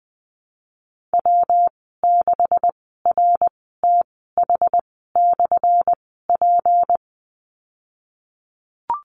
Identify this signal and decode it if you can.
Morse code, heard as a keyed tone, about 19 wpm, spaced as for 14 wpm, 700 Hz. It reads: W6RTH/P
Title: 179 Lesson 29g - Slash - Call Signs 20wpm